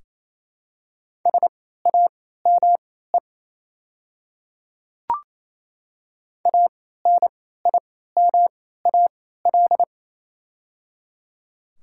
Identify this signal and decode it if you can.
Morse code, heard as a keyed tone, about 28 wpm, spaced as for 16 wpm, 700 Hz. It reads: SAME ANIMAL
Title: 3x Character Spacing - Top 200 Words - 28wpm